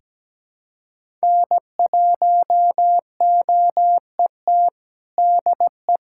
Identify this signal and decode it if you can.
Morse code, heard as a keyed tone, about 17 wpm, 700 Hz. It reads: N1OET DE
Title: USA Callsign DE Callsign - 5 Characters - 17wpm